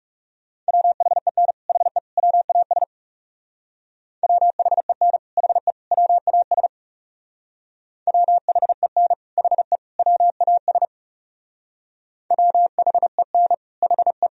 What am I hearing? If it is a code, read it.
Morse code, heard as a keyed tone, about 38 wpm, 700 Hz. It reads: WHEN HE WAS WHEN HE WAS WHEN HE WAS WHEN HE
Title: Speed-Racing - Sets of 3 Words 30wpm